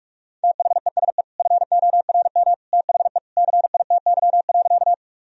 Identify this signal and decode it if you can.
Morse code, heard as a keyed tone, about 41 wpm, 700 Hz. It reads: THESE FORM THE CITY.
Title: Sentences from Top 1000 Words - Review Part 1 - 45wpm